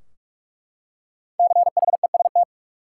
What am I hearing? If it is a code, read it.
Morse code, heard as a keyed tone, about 41 wpm, 700 Hz. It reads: CHEST
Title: Five Letter Words - 45wpm